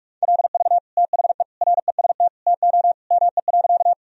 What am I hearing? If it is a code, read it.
Morse code, heard as a keyed tone, about 41 wpm, 700 Hz. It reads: LV THE REST TO ME.
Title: Sentences from Top 500 Words and Abbreviations 45wpm